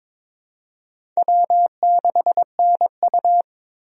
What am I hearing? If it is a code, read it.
Morse code, heard as a keyed tone, about 21 wpm, 700 Hz. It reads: W6NU
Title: USA Call Signs - 4 Characters - Repeated 3x - 22wpm